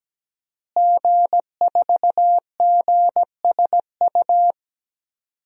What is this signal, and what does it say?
Morse code, heard as a keyed tone, about 17 wpm, 700 Hz. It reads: G4GSU
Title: DXCC-100 Call Signs - 5 Characters - Repeated 3x - 17wpm